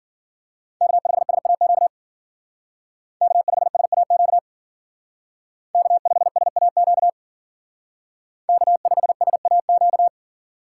Morse code, heard as a keyed tone, about 52 wpm, 700 Hz. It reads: K5SAQ K5SAQ K5SAQ K5SAQ